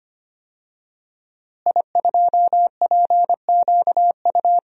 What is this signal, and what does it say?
Morse code, heard as a keyed tone, about 25 wpm, 700 Hz. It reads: I2PQU
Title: DXCC-100 Call Signs - 5 Characters - Repeated 3x - 25wpm